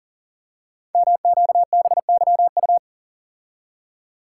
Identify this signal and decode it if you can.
Morse code, heard as a keyed tone, about 39 wpm, 700 Hz. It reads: MQBYU
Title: Code Groups - 5 Letters - 40wpm